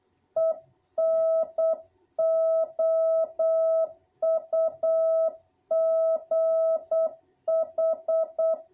Morse code, heard as a keyed tone, about 8 wpm, 660 Hz. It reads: ENOUGH
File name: jh8cbh / 2021cw